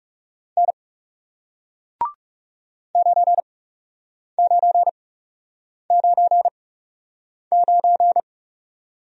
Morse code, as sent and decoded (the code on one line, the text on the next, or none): -. / ----. ----. ----. ----.
N 9999